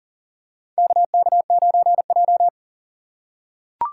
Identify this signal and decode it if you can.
Morse code, heard as a keyed tone, about 40 wpm, 700 Hz. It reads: KK9J
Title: USA Call Signs - 4 Characters - 6x Character Spacing - 40wpm